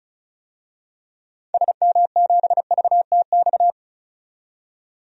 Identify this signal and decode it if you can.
Morse code, heard as a keyed tone, about 34 wpm, 700 Hz. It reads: SM7VTX
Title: DXCC-100 Call Signs - 6 Characters - 35wpm